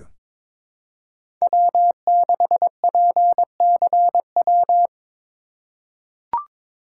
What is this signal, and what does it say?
Morse code, heard as a keyed tone, about 21 wpm, 700 Hz. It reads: W6PCW
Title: USA Call Signs - 5 Characters - 4x Character Spacing - 22wpm